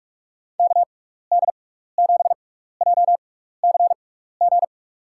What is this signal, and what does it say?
Morse code, heard as a keyed tone, about 42 wpm, spaced as for 15 wpm, 700 Hz. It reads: KD7JCG
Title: USA Call Signs - 6x Character Spacing - 45wpm